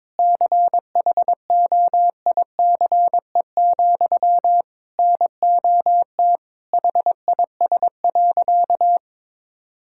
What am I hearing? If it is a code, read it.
Morse code, heard as a keyed tone, about 22 wpm, 700 Hz. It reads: CHOICE, NOT HIS.
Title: Sentences from Top 1000 Words - Review Part 6 - 22wpm